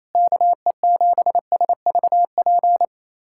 Morse code, heard as a keyed tone, about 27 wpm, 700 Hz. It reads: KE7SVP
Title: USA Callsign DE Callsign - 6 Characters - 28wpm